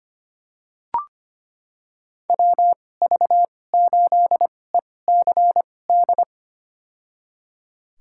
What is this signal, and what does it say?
Morse code, heard as a keyed tone, about 25 wpm, spaced as for 18 wpm, 700 Hz. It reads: WV8ECD
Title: USA Call Signs - 6 Characters - 2x Character Spacing - 25wpm